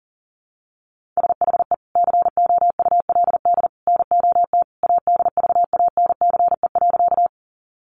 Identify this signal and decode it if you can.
Morse code, heard as a keyed tone, about 40 wpm, 700 Hz. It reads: SHE COULD NOT ADVANCE.
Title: Sentences from Top 1400 Words 40wpm